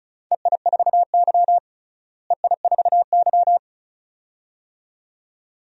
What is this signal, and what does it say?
Morse code, heard as a keyed tone, about 36 wpm, 700 Hz. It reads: EI4Y EI4Y
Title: DXCC-100 Call Signs - 4 Characters - Repeated 2x - 35wpm